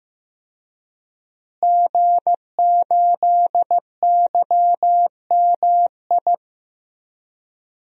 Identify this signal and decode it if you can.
Morse code, heard as a keyed tone, about 14 wpm, 700 Hz. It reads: G8YMI
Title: DXCC-100 Call Signs - 5 Characters - Repeated 3x - 15wpm